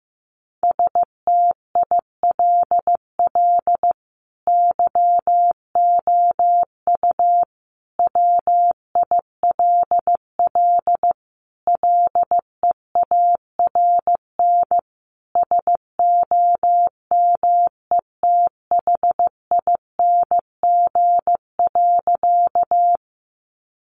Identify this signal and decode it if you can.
Morse code, heard as a keyed tone, about 15 wpm, 700 Hz. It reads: STILL YOU WILL LEARN SOMETHING.
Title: Sentences from Top 600 Words 15wpm